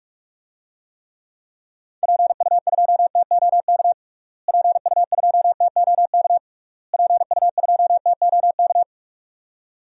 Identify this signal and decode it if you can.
Morse code, heard as a keyed tone, about 46 wpm, 700 Hz. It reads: PU2TOK PU2TOK PU2TOK
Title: DXCC-100 Call Signs - 6 Characters - Repeated 3x - 45wpm